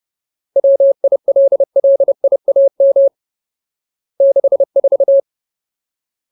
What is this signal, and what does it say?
Morse code, heard as a keyed tone, about 30 wpm, 550 Hz. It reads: WILLIAM 64
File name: Ultimate Word List / 30WPM Ultimate